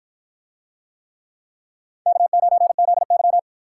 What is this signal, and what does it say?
Morse code, heard as a keyed tone, about 50 wpm, 700 Hz. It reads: K9ZY